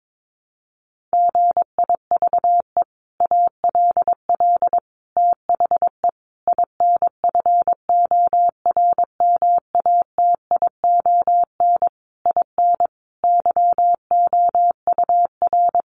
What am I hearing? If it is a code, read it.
Morse code, heard as a keyed tone, about 22 wpm, 700 Hz. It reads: GIVE ALL THE INFORMATION IN YOUR
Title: Sentences from Top 600 Words 22wpm